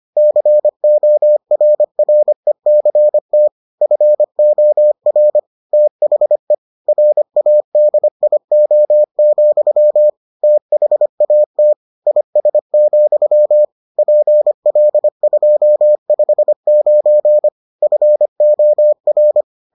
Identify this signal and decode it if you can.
Morse code, heard as a keyed tone, about 25 wpm, 600 Hz. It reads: CORRECT FOR THE RADIO, THAT IS, PL259 FOR